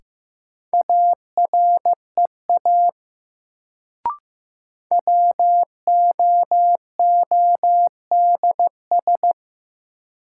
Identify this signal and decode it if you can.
Morse code, heard as a keyed tone, about 15 wpm, 700 Hz. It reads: AREA WOODS
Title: Sentences from Top 1000 Words - Review Part 2 - New Word Drill - 15wpm